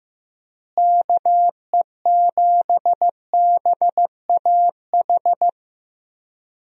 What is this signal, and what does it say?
Morse code, heard as a keyed tone, about 15 wpm, 700 Hz. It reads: KE7BAH
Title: USA Call Signs - 6 Characters - Repeated 2x - 15wpm